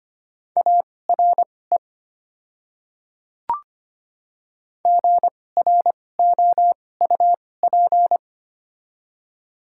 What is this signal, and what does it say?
Morse code, heard as a keyed tone, about 25 wpm, spaced as for 18 wpm, 700 Hz. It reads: ARE GROUP
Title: 2x Character Spacing - Top 300 Words - 25wpm